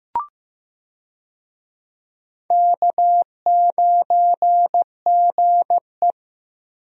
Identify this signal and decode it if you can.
Morse code, heard as a keyed tone, about 15 wpm, 700 Hz. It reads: K9GE
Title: USA Call Signs - 4 Characters - Repeated 3x - 15wpm